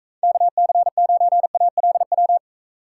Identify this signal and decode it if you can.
Morse code, heard as a keyed tone, about 42 wpm, 700 Hz. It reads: KC9ALW